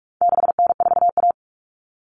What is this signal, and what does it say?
Morse code, heard as a keyed tone, about 44 wpm, 700 Hz. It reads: 6N4A
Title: Common Cut Numbers - 4-Digit 45wpm